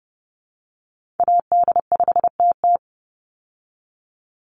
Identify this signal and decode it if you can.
Morse code, heard as a keyed tone, about 28 wpm, 700 Hz. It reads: AD5TT
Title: Call Signs Repeated 3x - 30wpm